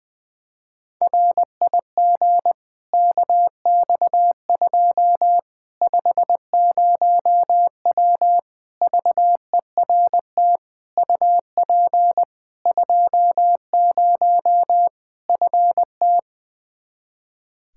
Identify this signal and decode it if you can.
Morse code, heard as a keyed tone, about 20 wpm, 700 Hz. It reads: RIG KX2 50W VERT UP 20 FT
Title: Practice QSO - Exchange 2 - Info - 20wpm